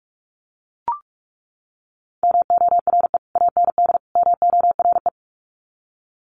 Morse code, heard as a keyed tone, about 44 wpm, 700 Hz. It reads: MORE AND MORE
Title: Sets of 3 Words 45wpm